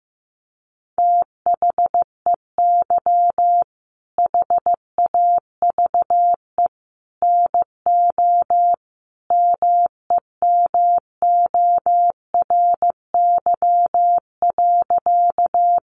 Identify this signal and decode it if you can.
Morse code, heard as a keyed tone, about 14 wpm, 700 Hz. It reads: THEY HAVE NO MEMORY.
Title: Sentences from Top 1300 Words 15wpm